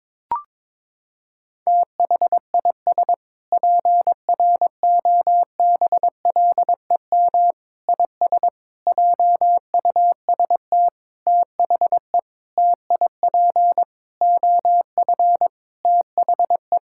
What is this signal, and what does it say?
Morse code, heard as a keyed tone, about 22 wpm, 700 Hz. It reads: THIS PROBLEM IS JUST THE TIP OF THE
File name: Sentences with 7-Word Phrases - 22wpm